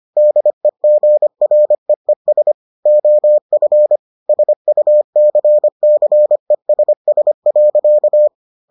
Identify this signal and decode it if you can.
Morse code, heard as a keyed tone, about 25 wpm, 600 Hz. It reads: DEGREES OF SUCCESS.